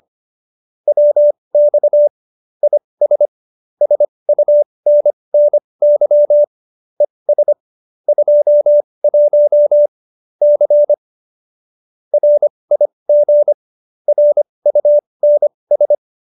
Morse code, heard as a keyed tone, about 25 wpm, spaced as for 20 wpm, 590 Hz. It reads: WX IS SUNNY ES 21 C RIG RUNS
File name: qso215b 20